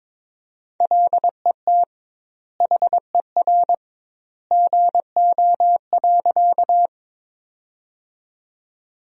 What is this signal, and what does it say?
Morse code, heard as a keyed tone, about 21 wpm, 700 Hz. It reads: LET HER GO.
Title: 2x Word-Spacing - Sentences from Top 300 Words - 22wpm